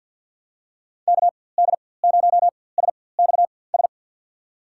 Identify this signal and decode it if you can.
Morse code, heard as a keyed tone, about 47 wpm, spaced as for 23 wpm, 700 Hz. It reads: KD0SXS
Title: USA Call Signs - 6 Characters - 4x Character Spacing - 50wpm